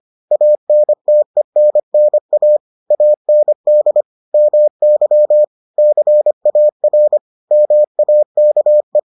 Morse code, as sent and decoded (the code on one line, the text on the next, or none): .- -. - . -. -. .- / .- -. -.. / -- -.-- / -.-. .- .-. / -- .- -.- .
ANTENNA AND MY CAR MAKE